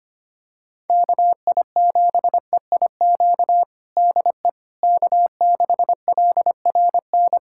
Morse code, heard as a keyed tone, about 25 wpm, 700 Hz. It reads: KI7EIQ DE K6LRN
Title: USA Callsign DE Callsign 25wpm